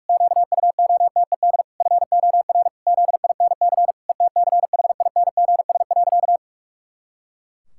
Morse code, heard as a keyed tone, about 46 wpm, 700 Hz. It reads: QUOTED FOR ZINC ETCHINGS.
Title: Pangram Challenge v2 - 45wpm